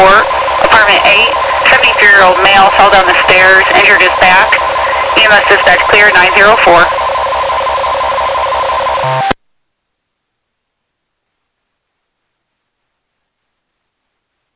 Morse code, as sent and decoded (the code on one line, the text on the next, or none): none